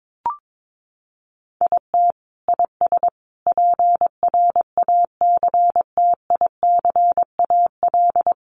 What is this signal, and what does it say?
Morse code, heard as a keyed tone, about 22 wpm, 700 Hz. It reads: IT IS PRACTICAL
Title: Sentences from Top 1400 Words 22wpm